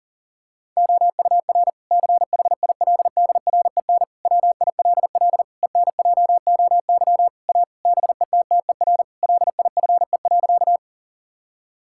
Morse code, heard as a keyed tone, about 40 wpm, 700 Hz. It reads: OUR CHILDREN WILL ENJOY A BETTER LIFE.
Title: Sentences from Top 1000 Words - Review Part 5 - 40wpm